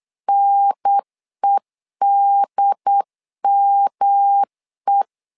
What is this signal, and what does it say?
Morse code, heard as a keyed tone, about 8 wpm, 800 Hz. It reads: NEDME
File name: LEZIONI / 3 MWRO / 3 M 8wpm 2300